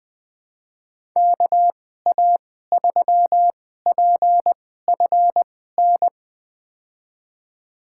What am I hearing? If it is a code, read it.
Morse code, heard as a keyed tone, about 20 wpm, spaced as for 14 wpm, 700 Hz. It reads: KA3PFN
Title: USA Call Signs - 6 Characters - 2x Character Spacing - 20wpm